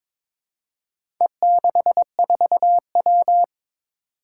Morse code, heard as a keyed tone, about 22 wpm, 700 Hz. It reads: E64W
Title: Four Challenging Letter-Number Combinations 22wpm